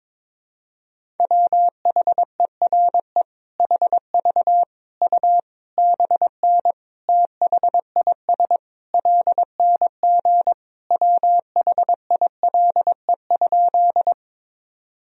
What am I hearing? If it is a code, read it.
Morse code, heard as a keyed tone, about 22 wpm, 700 Hz. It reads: WHERE HV U BN THIS LNG WHILE?
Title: Sentences from Top 300 Words and Abbreviations 22wpm